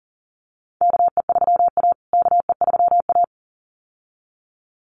Morse code, heard as a keyed tone, about 37 wpm, 700 Hz. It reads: KE3A KE3A
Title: USA Call Signs Repeated 2x 40wpm